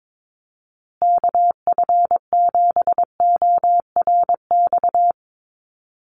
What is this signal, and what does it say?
Morse code, heard as a keyed tone, about 22 wpm, 700 Hz. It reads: KF7ORX
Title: USA Call Signs Repeated 3x 22wpm